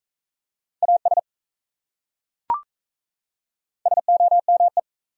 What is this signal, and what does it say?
Morse code, heard as a keyed tone, about 37 wpm, 700 Hz. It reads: AS SOME